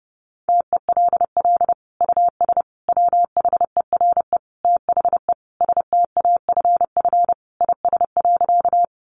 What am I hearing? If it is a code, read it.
Morse code, heard as a keyed tone, about 30 wpm, 700 Hz. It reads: TELL US WHERE THE STAFF IS.
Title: Sentences from Top 1200 Words 30wpm